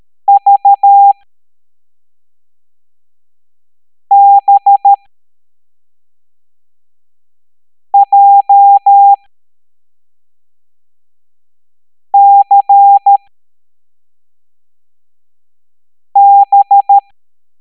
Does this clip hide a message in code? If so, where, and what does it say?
Yes: Morse code from 0.3 to 17.0 s, heard as a keyed tone, about 13 wpm, spaced as for 3 wpm, 800 Hz. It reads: VBJCB